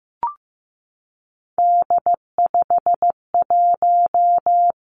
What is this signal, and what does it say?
Morse code, heard as a keyed tone, about 15 wpm, 700 Hz. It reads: D51
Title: Three Challenging Letter-Number Combinations 15wpm